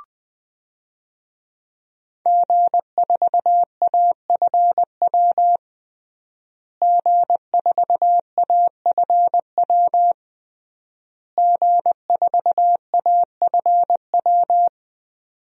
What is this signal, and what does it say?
Morse code, heard as a keyed tone, about 20 wpm, 700 Hz. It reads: G4AFW G4AFW G4AFW
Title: DXCC-100 Call Signs - 5 Characters - Repeated 3x - 20wpm